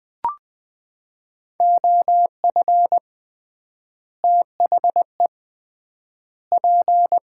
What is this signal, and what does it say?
Morse code, heard as a keyed tone, about 20 wpm, 700 Hz. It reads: OF THE P
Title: 3x Word-Spacing - Sets of 4 Words - 20wpm